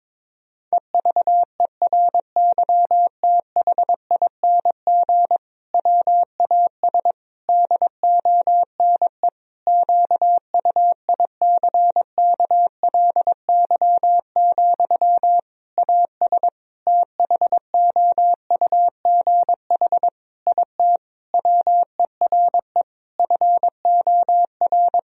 Morse code, heard as a keyed tone, about 22 wpm, 700 Hz. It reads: EVERYTHING WAS DONE QUICKLY, AS THOUGH IT WERE FOR
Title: Sentences from Top 1000 Words - Review Part 2 - 22wpm